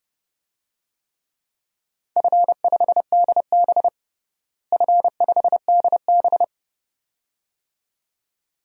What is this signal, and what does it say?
Morse code, heard as a keyed tone, about 30 wpm, 700 Hz. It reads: F5DB F5DB
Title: DXCC-100 Call Signs - 4 Characters - Repeated 2x - 30wpm